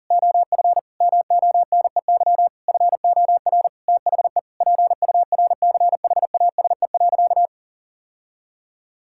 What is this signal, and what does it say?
Morse code, heard as a keyed tone, about 40 wpm, 690 Hz. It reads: OF MONEY FOR THE PURCHASE.
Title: Sentences from Top 1000 Words - Review Part 5 - 40wpm